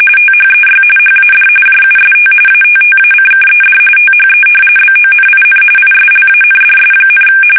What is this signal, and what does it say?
RTTY, about 45.45 baud. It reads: THEY HAD JUST ATTACKED THE BONIN ISLANDS,
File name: fsk850 / battleshipnj